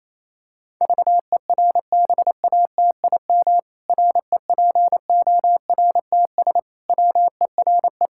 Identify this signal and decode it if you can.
Morse code, heard as a keyed tone, about 28 wpm, 700 Hz. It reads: VERBATIM REPORTS WERE
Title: Pangram Challenge v2 - 28wpm